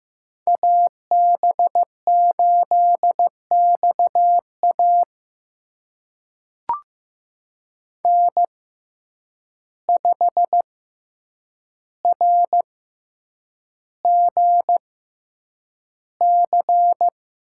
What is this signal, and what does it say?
Morse code, heard as a keyed tone, about 15 wpm, spaced as for 8 wpm, 700 Hz. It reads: AB8XA N5RGC
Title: USA Call Signs - 5 Characters - 6x Character Spacing - 15wpm